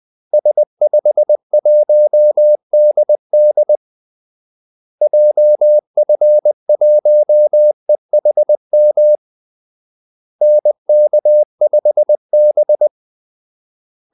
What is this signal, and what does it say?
Morse code, heard as a keyed tone, about 20 wpm, 600 Hz. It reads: S51DD JF1EHM NK5B
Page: Kaldesignaler 20 wpm